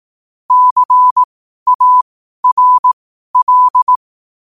Morse code, heard as a keyed tone, about 18 wpm, spaced as for 12 wpm, 1000 Hz. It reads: CARL
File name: qotdfiles / 12wpm